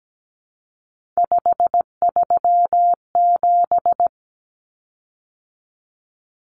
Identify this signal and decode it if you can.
Morse code, heard as a keyed tone, about 17 wpm, 700 Hz. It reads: 537
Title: Common Cut Numbers - 3-Digit 17wpm